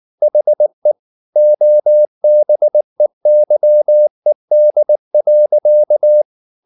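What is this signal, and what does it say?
Morse code, heard as a keyed tone, about 19 wpm, 600 Hz. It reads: HE OBEYED.
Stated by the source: Morse code at 19 wpm from CW med Gnister, reading Ken Follett's novel Never in English.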